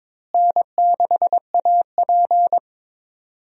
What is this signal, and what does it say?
Morse code, heard as a keyed tone, about 22 wpm, 700 Hz. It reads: N6AP
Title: USA Call Signs - 4 Characters - Repeated 2x - 22wpm